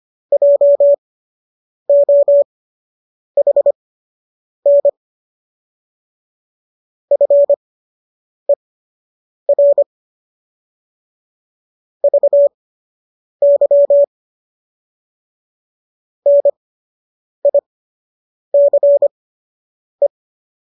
Morse code, heard as a keyed tone, about 25 wpm, spaced as for 8 wpm, 570 Hz. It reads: JOHN FER VY NICE